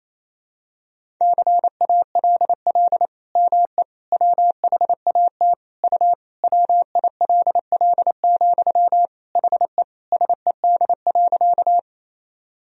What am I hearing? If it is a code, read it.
Morse code, heard as a keyed tone, about 28 wpm, 700 Hz. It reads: CALL ME WHAT U WILL, HE SED.
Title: Sentences from Top 200 Words and Abbreviations 28wpm